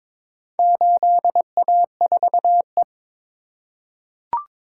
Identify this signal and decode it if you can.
Morse code, heard as a keyed tone, about 22 wpm, 700 Hz. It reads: 8A4E
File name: practice / 22wpm / Four Letter-Number Combinations - 22wpm